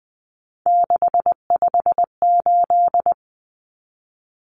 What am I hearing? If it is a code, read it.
Morse code, heard as a keyed tone, about 20 wpm, 700 Hz. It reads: 658
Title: Numbers - 3-Digit 20wpm